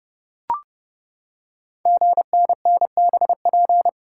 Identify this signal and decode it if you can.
Morse code, heard as a keyed tone, about 28 wpm, 700 Hz. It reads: GNNBP
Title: Code Groups - 5 Letters - 30wpm